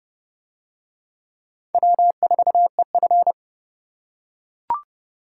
Morse code, heard as a keyed tone, about 28 wpm, 700 Hz. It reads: W4EF